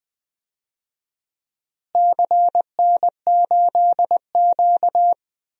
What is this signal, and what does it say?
Morse code, heard as a keyed tone, about 20 wpm, 700 Hz. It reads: CN8Q
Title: DXCC-100 Call Signs - 4 Characters - Repeated 2x - 20wpm